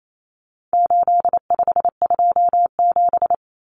Morse code, heard as a keyed tone, about 28 wpm, 700 Hz. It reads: 8527
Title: Common Cut Numbers - 4-Digit 28wpm